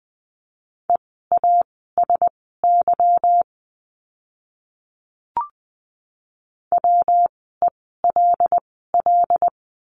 Morse code, heard as a keyed tone, about 20 wpm, spaced as for 14 wpm, 700 Hz. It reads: EASY WELL
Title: 146 Lesson 24f - Letter y - Words 20wpm